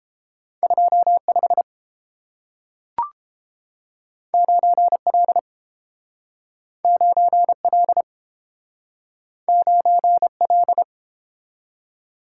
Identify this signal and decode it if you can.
Morse code, heard as a keyed tone, about 30 wpm, 700 Hz. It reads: 25 9L 9L 9L